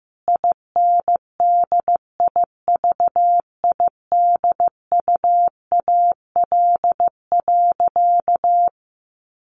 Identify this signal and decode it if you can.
Morse code, heard as a keyed tone, about 15 wpm, 700 Hz. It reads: INDIVIDUAL.
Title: Sentences from Top 1500 Words 15wpm